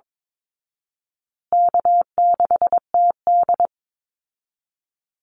Morse code, heard as a keyed tone, about 22 wpm, 700 Hz. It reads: K6TD
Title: Call Signs Repeated 3x - 22wpm